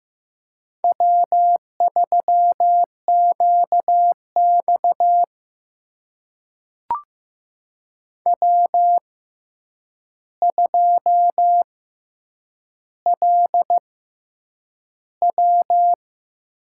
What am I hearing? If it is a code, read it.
Morse code, heard as a keyed tone, about 15 wpm, spaced as for 8 wpm, 700 Hz. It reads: W3QX W2LW
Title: USA Call Signs - 4 Characters - 6x Character Spacing - 15wpm